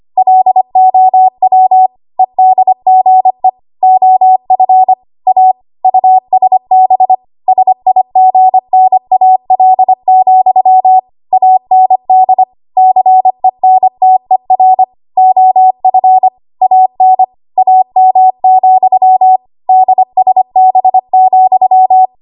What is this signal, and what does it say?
Morse code, heard as a keyed tone, about 25 wpm, 750 Hz. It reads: LOW EDGE OF A USB SIGNAL, AND CENTER OF AN AM, DSB,
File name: Archive / 25 WPM / 25WPM